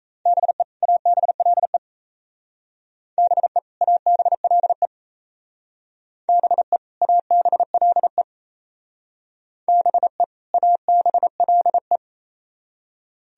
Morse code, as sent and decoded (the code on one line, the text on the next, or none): -... / .- -... .-.. / -... / .- -... .-.. / -... / .- -... .-.. / -... / .- -... .-.. .
BE ABLE BE ABLE BE ABLE BE ABLE